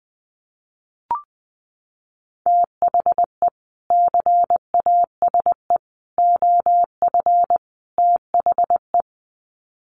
Morse code, heard as a keyed tone, about 20 wpm, 700 Hz. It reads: THE CASE OF THE
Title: Sets of 4 Words 20wpm